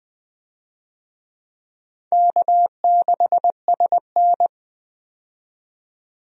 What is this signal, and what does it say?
Morse code, heard as a keyed tone, about 20 wpm, 700 Hz. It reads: K6SN
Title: USA Call Signs - 4 Characters - 20wpm